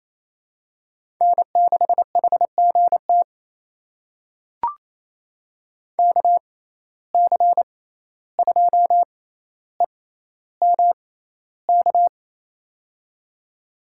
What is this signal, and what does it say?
Morse code, heard as a keyed tone, about 28 wpm, spaced as for 14 wpm, 700 Hz. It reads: N6HGT KC2EMK
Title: USA Call Signs - 6x Character Spacing - 28wpm